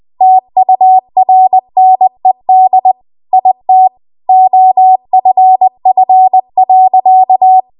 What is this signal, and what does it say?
Morse code, heard as a keyed tone, about 20 wpm, 750 Hz. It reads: TURNED IT OFF.
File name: Archive / 20 WPM / 20WPM